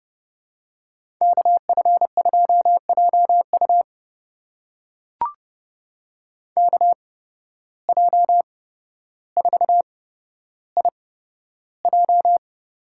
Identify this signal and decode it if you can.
Morse code, heard as a keyed tone, about 30 wpm, spaced as for 13 wpm, 700 Hz. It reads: KF2JU KJ4IJ
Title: USA Call Signs - 5 Characters - 8x Character Spacing - 30wpm